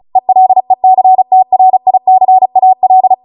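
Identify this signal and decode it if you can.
Morse code, heard as a keyed tone, about 36 wpm, 750 Hz. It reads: ELECTRICAL
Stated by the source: Amateur radio Morse code practice archive transmission at 35 wpm.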